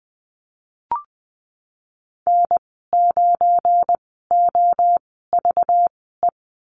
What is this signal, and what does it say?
Morse code, heard as a keyed tone, about 20 wpm, spaced as for 15 wpm, 700 Hz. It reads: N9OVE
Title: 193 Lesson 31g - Letter v - Call Signs 20wpm